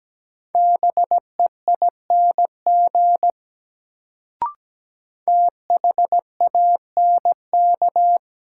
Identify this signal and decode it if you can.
Morse code, heard as a keyed tone, about 17 wpm, 700 Hz. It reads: BEING THANK
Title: Sentences from Top 600 Words - New Word Drill - 17wpm